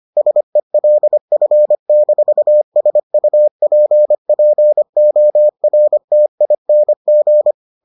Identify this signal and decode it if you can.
Morse code, heard as a keyed tone, about 25 wpm, 600 Hz. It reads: SELF-SUPPORTING